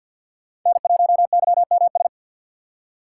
Morse code, heard as a keyed tone, about 52 wpm, 700 Hz. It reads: N1YMS
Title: USA Call Signs - 5 Characters - 6x Character Spacing - 50wpm